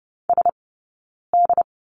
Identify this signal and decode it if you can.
Morse code, heard as a keyed tone, about 30 wpm, spaced as for 9 wpm, 690 Hz. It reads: SD
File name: course / 30:10wpm / 086 Lesson 15d - Number 5 - Two Characters 30wpm